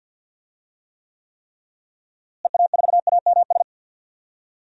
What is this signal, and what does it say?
Morse code, heard as a keyed tone, about 47 wpm, 700 Hz. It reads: EA4AMS
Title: DXCC-100 Call Signs - 6 Characters - Repeated 3x - 50wpm